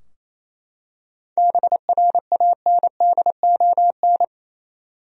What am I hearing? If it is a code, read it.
Morse code, heard as a keyed tone, about 27 wpm, 700 Hz. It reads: BRANDON
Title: USA Common Names - 28wpm